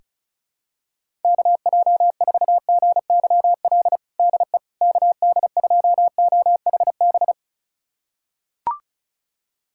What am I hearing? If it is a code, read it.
Morse code, heard as a keyed tone, about 34 wpm, 700 Hz. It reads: KJ4GYL DE KD2OHB
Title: USA Callsign DE Callsign - 6 Characters - 35wpm